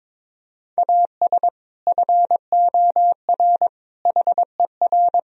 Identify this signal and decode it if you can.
Morse code, heard as a keyed tone, about 22 wpm, 700 Hz. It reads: AS FOR HER
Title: Sentences from Top 1000 Words - Review Part 2 - 22wpm